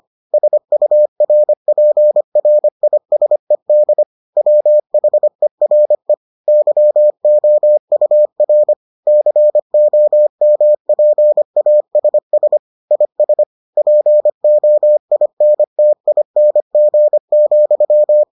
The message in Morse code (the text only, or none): SURPRISED WHERE YOUR COMPASS IS POINTING,